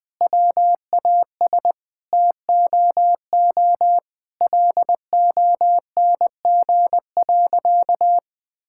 Morse code, heard as a keyed tone, about 20 wpm, 700 Hz. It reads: WAS TOO LONG.
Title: Sentences from Top 1000 Words - Review Part 5 - 20wpm